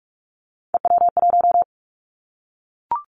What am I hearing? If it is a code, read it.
Morse code, heard as a keyed tone, about 44 wpm, 700 Hz. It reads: EW1
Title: Callsign Prefixes for Top 100 DXCC - 45wpm